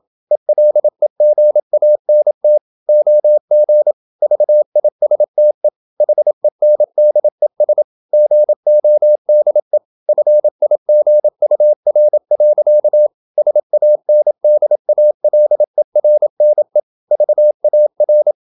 Morse code, heard as a keyed tone, about 27 wpm, 600 Hz. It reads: ELEGANT OG VISTE HENDES GODE FIGUR. SANDALERNE VAR
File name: Ferie QRQ 27wpm